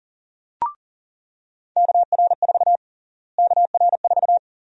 Morse code, heard as a keyed tone, about 40 wpm, 700 Hz. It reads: KR4 KR4
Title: USA Call Sign Prefixes - Repeated 2x - 40wpm